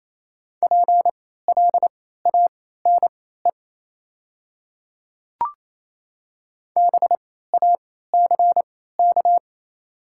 Morse code, heard as a keyed tone, about 28 wpm, spaced as for 16 wpm, 700 Hz. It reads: PLANE BACK
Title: 3x Character Spacing - Top 500 Words - 28wpm